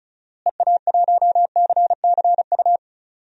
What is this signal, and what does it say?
Morse code, heard as a keyed tone, about 36 wpm, 700 Hz. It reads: EA1CCU
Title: DXCC-100 Call Signs - 6 Characters - Repeated 3x - 35wpm